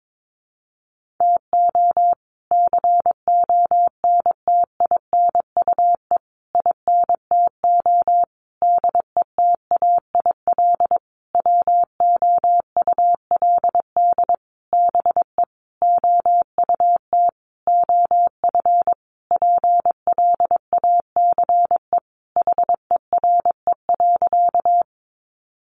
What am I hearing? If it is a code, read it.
Morse code, heard as a keyed tone, about 22 wpm, 700 Hz. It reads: TO CONTINUE INTO DETAIL WOULD BE OUT OF PLACE HERE.
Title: Sentences from Top 1500 Words 22wpm